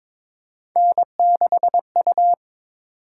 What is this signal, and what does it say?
Morse code, heard as a keyed tone, about 22 wpm, 700 Hz. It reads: N6U
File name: practice / 22wpm / Three Letter-Number Combinations - 22wpm